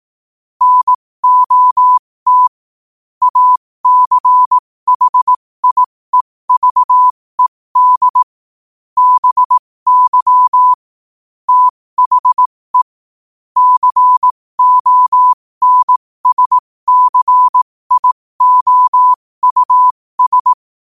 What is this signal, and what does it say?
Morse code, heard as a keyed tone, about 18 wpm, spaced as for 15 wpm, 1000 Hz. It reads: NOT ACHIEVED BY THE CONSCIOUS
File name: qotdfiles / 15wpm